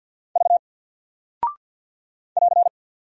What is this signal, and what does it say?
Morse code, heard as a keyed tone, about 49 wpm, 700 Hz. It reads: V <AR>
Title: Single Character (Letter:Number:Punctuation:Prosign) 50wpm